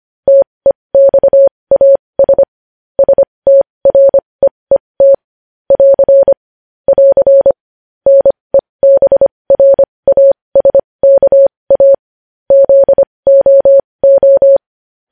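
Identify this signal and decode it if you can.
Morse code, heard as a keyed tone, about 25 wpm, spaced as for 20 wpm, 550 Hz. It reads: TEXAS STREET <AR> <AR> NEBRASKA ZOO
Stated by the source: Morse practice of news headlines at 20 wpm.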